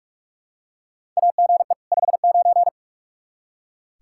Morse code, heard as a keyed tone, about 46 wpm, 700 Hz. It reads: AGE 59
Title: Practice QSO - Exchange 3 - More Info - 45wpm